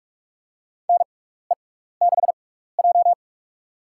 Morse code, heard as a keyed tone, about 46 wpm, spaced as for 16 wpm, 700 Hz. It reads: NE6J
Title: USA Call Signs - 4 Characters - 6x Character Spacing - 45wpm